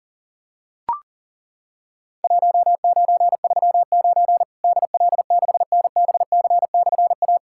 Morse code, heard as a keyed tone, about 40 wpm, 700 Hz. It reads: 1939 DL6NBC/A
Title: Worked All Europe DX Contest - QTC Exchange - 40wpm